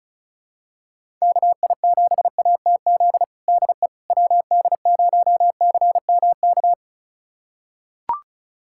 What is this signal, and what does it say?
Morse code, heard as a keyed tone, about 36 wpm, 700 Hz. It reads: KI7ATZ DE WD0CMK
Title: USA Callsign DE Callsign - 6 Characters - 35wpm